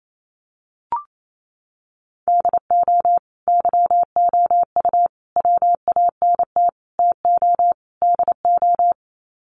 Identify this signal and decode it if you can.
Morse code, heard as a keyed tone, about 28 wpm, 700 Hz. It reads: DO YOU WANT TO DO
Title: Sets of 5 Words Encore 28wpm